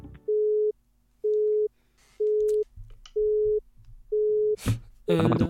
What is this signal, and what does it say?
Telephone call progress tone (Europe, 425 Hz), heard as a telephone network's tone, about 420 Hz, with speech. Busy tone.